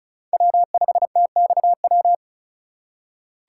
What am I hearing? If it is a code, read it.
Morse code, heard as a keyed tone, about 36 wpm, 700 Hz. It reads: W5TXW
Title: USA Call Signs - 5 Characters - 2x Character Spacing - 35wpm